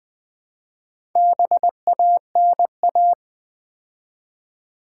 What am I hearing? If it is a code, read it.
Morse code, heard as a keyed tone, about 20 wpm, 700 Hz. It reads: BANA
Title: Build A Word - 2 Letters At A Time - 6-Letter Words - 20wpm